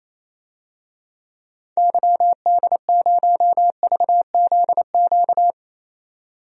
Morse code, heard as a keyed tone, about 28 wpm, 700 Hz. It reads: YD0VZQ
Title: DXCC-100 Call Signs - 6 Characters - 28wpm